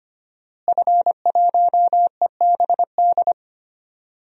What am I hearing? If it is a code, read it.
Morse code, heard as a keyed tone, about 25 wpm, 700 Hz. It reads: F1EBD
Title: DXCC-100 Call Signs - 5 Characters - Repeated 3x - 25wpm